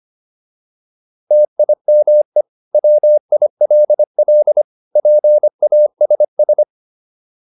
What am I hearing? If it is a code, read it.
Morse code, heard as a keyed tone, about 25 wpm, 600 Hz. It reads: TIME WILL PASS